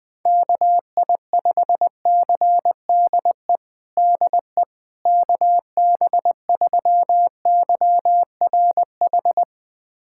Morse code, heard as a keyed tone, about 20 wpm, 700 Hz. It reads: KI5CDE DE KB3YRH
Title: USA Callsign DE Callsign 20wpm